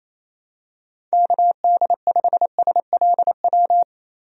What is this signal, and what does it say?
Morse code, heard as a keyed tone, about 28 wpm, 700 Hz. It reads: KD5SLW
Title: USA Call Signs - 6 Characters - 28wpm